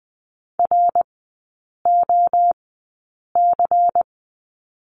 Morse code, heard as a keyed tone, about 20 wpm, spaced as for 8 wpm, 700 Hz. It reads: ROC
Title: 093 Lesson 16e - Letter c - Three Characters 20wpm